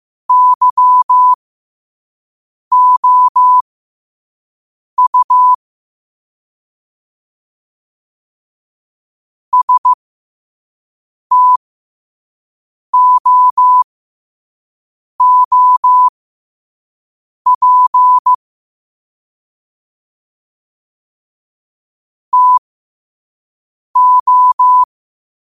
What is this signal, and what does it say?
Morse code, heard as a keyed tone, about 15 wpm, spaced as for 5 wpm, 1000 Hz. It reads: YOU STOOP TO
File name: qotdfiles / 5wpm